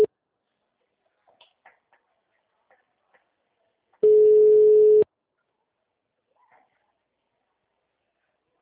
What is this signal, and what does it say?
Ringback tone (Europe, 425 Hz).